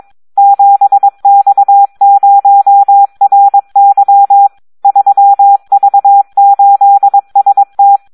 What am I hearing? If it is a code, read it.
Morse code, heard as a keyed tone, about 22 wpm, 800 Hz. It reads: 7X0RY 3V8ST
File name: Audio practica CW / SD 1 22